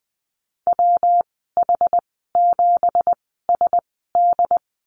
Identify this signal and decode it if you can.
Morse code, heard as a keyed tone, about 20 wpm, spaced as for 14 wpm, 700 Hz. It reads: WH7SD
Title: 193 Lesson 31g - Letter v - Call Signs 20wpm